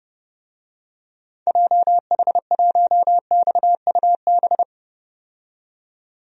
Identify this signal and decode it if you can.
Morse code, heard as a keyed tone, about 30 wpm, 700 Hz. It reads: JH1XUB
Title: DXCC-100 Call Signs - 6 Characters - 30wpm